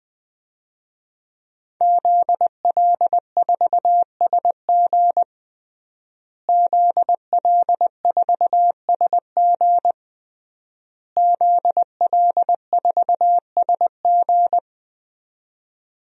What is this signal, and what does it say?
Morse code, heard as a keyed tone, about 20 wpm, 700 Hz. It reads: ZL4SG ZL4SG ZL4SG
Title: DXCC-100 Call Signs - 5 Characters - Repeated 3x - 20wpm